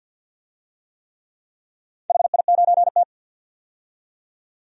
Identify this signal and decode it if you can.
Morse code, heard as a keyed tone, about 47 wpm, 700 Hz. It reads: HI9T
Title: DXCC-100 Call Signs - 4 Characters - 50wpm